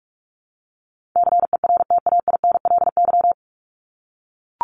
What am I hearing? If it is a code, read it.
Morse code, heard as a keyed tone, about 44 wpm, 700 Hz. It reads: CERTAINLY
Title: LY Suffix - Top 100 Words with Base Words 45wpm